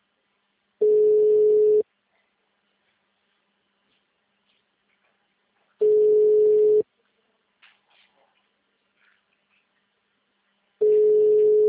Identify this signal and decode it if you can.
Ringback tone (Europe, 425 Hz).